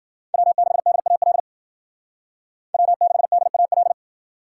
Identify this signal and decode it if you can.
Morse code, heard as a keyed tone, about 51 wpm, 700 Hz. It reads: W6DAL W6DAL